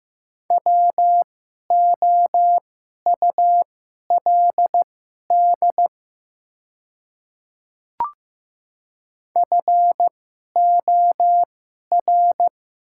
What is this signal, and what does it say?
Morse code, heard as a keyed tone, about 15 wpm, spaced as for 11 wpm, 700 Hz. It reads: WOULD FOR
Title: 2x Character Spacing - Top 500 Words - 15wpm